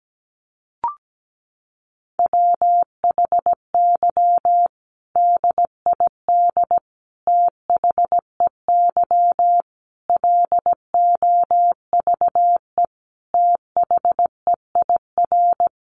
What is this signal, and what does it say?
Morse code, heard as a keyed tone, about 17 wpm, 700 Hz. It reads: WHY DID THEY LOVE THEIR
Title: Sentences from Top 1500 Words 17wpm